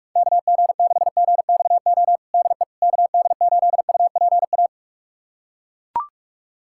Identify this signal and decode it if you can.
Morse code, heard as a keyed tone, about 41 wpm, 700 Hz. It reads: KG6GXO DE KD8UPA
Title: USA Callsign DE Callsign - 6 Characters - 45wpm